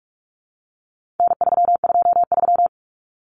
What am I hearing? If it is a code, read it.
Morse code, heard as a keyed tone, about 44 wpm, 700 Hz. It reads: N323
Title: Common Cut Numbers - 4-Digit 45wpm